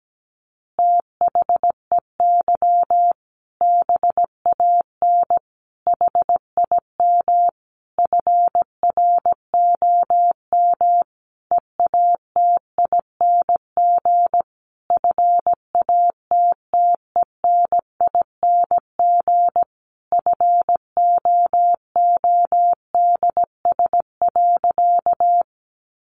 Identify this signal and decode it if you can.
Morse code, heard as a keyed tone, about 17 wpm, 700 Hz. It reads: THEY BAN HIM FROM EATING FATTENING FOODS.
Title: Easy Phrases 17wpm